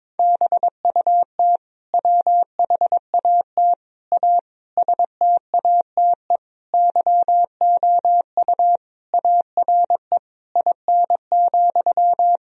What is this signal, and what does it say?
Morse code, heard as a keyed tone, about 22 wpm, 700 Hz. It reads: BUT WHAT A STATE YOU ARE IN,
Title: Sentences from Top 1000 Words - Review Part 2 - 22wpm